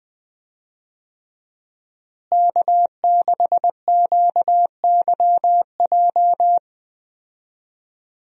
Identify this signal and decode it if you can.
Morse code, heard as a keyed tone, about 20 wpm, 700 Hz. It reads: K6QYJ
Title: USA Call Signs - 5 Characters - Repeated 3x - 20wpm